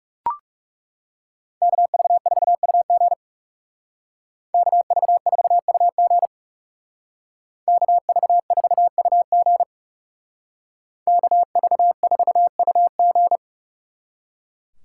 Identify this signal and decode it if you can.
Morse code, heard as a keyed tone, about 36 wpm, 700 Hz. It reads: KV4UG KV4UG KV4UG KV4UG
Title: Speed Racing - USA 2x2 Call Signs 30wpm